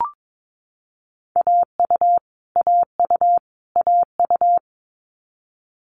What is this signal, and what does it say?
Morse code, heard as a keyed tone, about 22 wpm, 700 Hz. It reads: AU AU AU
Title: Multi-letter Phonemes Repeated 3x 22wpm